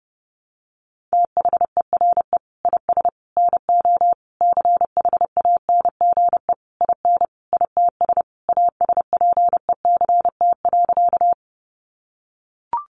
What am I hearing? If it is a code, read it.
Morse code, heard as a keyed tone, about 30 wpm, 700 Hz. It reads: THERE IS NO CHANGE IN ITS ASPECT.
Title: Sentences from Top 1300 Words 30wpm